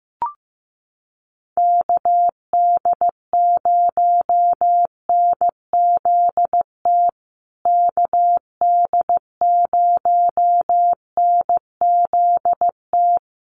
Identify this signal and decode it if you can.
Morse code, heard as a keyed tone, about 15 wpm, 700 Hz. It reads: KD0NZT KD0NZT
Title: USA Call Signs Repeated 3x 15wpm